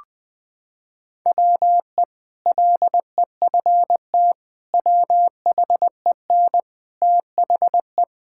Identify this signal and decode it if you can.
Morse code, heard as a keyed tone, about 19 wpm, 700 Hz. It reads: WE LEFT WHEN THE
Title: Sentences with 2-Word Phrases - 20wpm